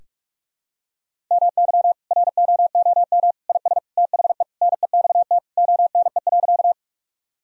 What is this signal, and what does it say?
Morse code, heard as a keyed tone, about 46 wpm, 700 Hz. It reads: MY ROOM IS THE NEXT ONE.
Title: Sentences from Top 1000 Words - Review Part 1 - 45wpm